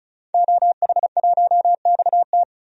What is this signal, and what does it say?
Morse code, heard as a keyed tone, about 34 wpm, 700 Hz. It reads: OH1XT
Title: DXCC-100 Call Signs - 5 Characters - Repeated 3x - 35wpm